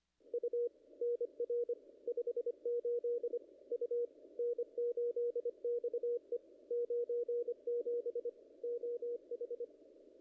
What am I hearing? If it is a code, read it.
Morse code, heard as a keyed tone, about 25 wpm, 410 Hz. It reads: U NR 58 U N8XE 97 OH
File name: contests / ss-cw-k5tr-op-n5rz -0100